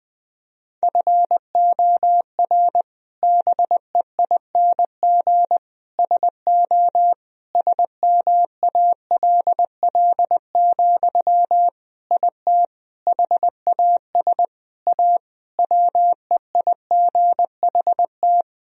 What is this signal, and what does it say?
Morse code, heard as a keyed tone, about 20 wpm, 700 Hz. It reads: FOR BEING SO SMALL, IT HAS A WEIGHT
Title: Sentences from Top 1000 Words - Review Part 3 - 20wpm